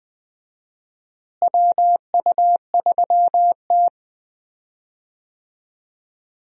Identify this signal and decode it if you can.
Morse code, heard as a keyed tone, about 20 wpm, 700 Hz. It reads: WU3T
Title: USA Call Signs - 4 Characters - Repeated 2x - 20wpm